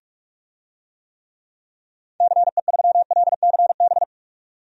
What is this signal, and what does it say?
Morse code, heard as a keyed tone, about 42 wpm, 700 Hz. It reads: CE3LCB
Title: DXCC-100 Call Signs - 6 Characters - 45wpm